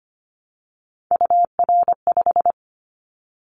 Morse code, heard as a keyed tone, about 25 wpm, 700 Hz. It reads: UR5
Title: Callsign Prefixes for Top 25 DXCC - Repeated 3x - 25wpm